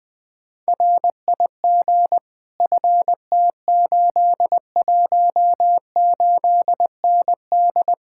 Morse code, heard as a keyed tone, about 20 wpm, 700 Hz. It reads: RIG FT818ND
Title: Speed-Racing Common QSO Phrases - 20wpm